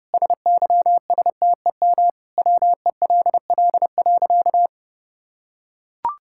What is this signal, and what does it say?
Morse code, heard as a keyed tone, about 32 wpm, 700 Hz. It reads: SYSTEM WELL.
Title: Sentences from Top 1000 Words - Review Part 5 - 30wpm